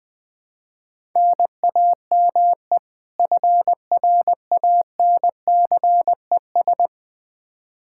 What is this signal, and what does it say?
Morse code, heard as a keyed tone, about 20 wpm, 700 Hz. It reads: NAME FRANCES
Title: Practice QSO - Exchange 1 - Beginning - 20wpm